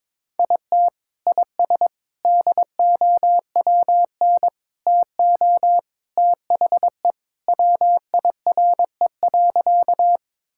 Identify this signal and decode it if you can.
Morse code, heard as a keyed tone, about 22 wpm, 700 Hz. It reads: IT IS DOWN TO THE WIRE.
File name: practice / 22wpm / Sentences with 3-Word Phrases - 22wpm